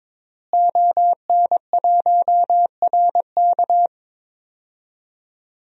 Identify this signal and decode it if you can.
Morse code, heard as a keyed tone, about 21 wpm, 700 Hz. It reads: ON1RK
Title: DXCC-100 Call Signs - 5 Characters - Repeated 3x - 22wpm